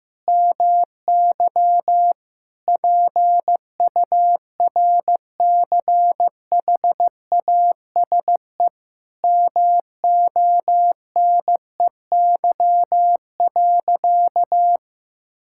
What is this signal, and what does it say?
Morse code, heard as a keyed tone, about 16 wpm, 700 Hz. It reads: MY PURCHASE MONEY.
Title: Sentences from Top 1000 Words - Review Part 5 - 15wpm